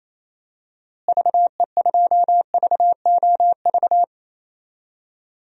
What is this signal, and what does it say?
Morse code, heard as a keyed tone, about 28 wpm, 700 Hz. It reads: VE2VOV
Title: DXCC-100 Call Signs - 6 Characters - Repeated 3x - 28wpm